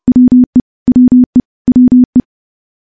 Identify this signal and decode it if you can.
Morse code, heard as a keyed tone, about 30 wpm, 250 Hz. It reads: WE WE WE
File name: morse-training-30WPM